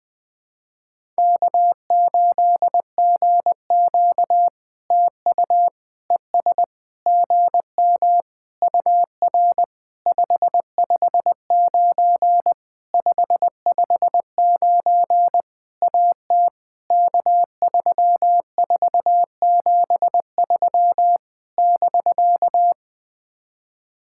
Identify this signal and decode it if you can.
Morse code, heard as a keyed tone, about 20 wpm, 700 Hz. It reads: K8GQ TU ES GM UR 559 559 AT K3473 <BK>
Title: POTA QSO Practice - 20wpm